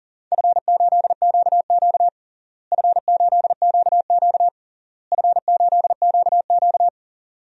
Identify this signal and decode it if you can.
Morse code, heard as a keyed tone, about 40 wpm, 700 Hz. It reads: F8QQ F8QQ F8QQ
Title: DXCC-100 Call Signs - 4 Characters - Repeated 3x - 40wpm